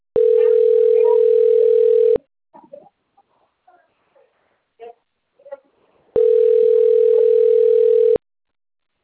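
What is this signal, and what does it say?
Ringback tone (North America), 440 + 480 Hz.
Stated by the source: Outgoing phone call recorded on a system labeled Limeira - Telemarket.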